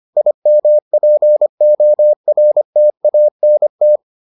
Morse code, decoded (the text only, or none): IMPORTANT